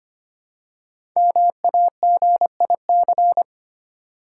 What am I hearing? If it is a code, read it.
Morse code, heard as a keyed tone, about 25 wpm, 700 Hz. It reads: MAGIC
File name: 5 Letter Words - First 3 Letters Given - 25wpm